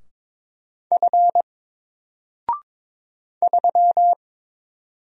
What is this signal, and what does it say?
Morse code, heard as a keyed tone, about 22 wpm, 700 Hz. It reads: F 3
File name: Single-Letter-Number-22wpm